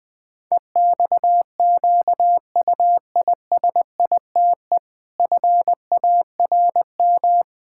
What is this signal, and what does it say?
Morse code, heard as a keyed tone, about 19 wpm, 700 Hz. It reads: EXQUISITE FARM
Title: Pangram Challenge v2 - 20wpm